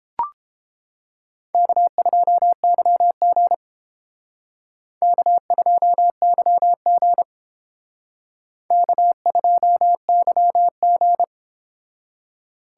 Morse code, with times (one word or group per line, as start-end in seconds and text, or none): 1.5-3.5 K2YG
5.0-7.2 K2YG
8.7-11.2 K2YG